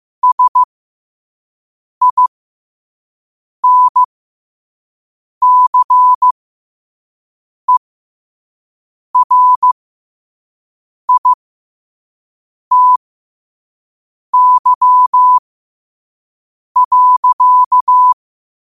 Morse code, heard as a keyed tone, about 15 wpm, spaced as for 5 wpm, 1000 Hz. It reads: SINCERITY.
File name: qotdfiles / 5wpm